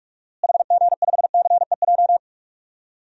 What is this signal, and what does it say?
Morse code, heard as a keyed tone, about 42 wpm, 700 Hz. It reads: HG5CEJ
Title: DXCC-100 Call Signs - 6 Characters - Repeated 3x - 45wpm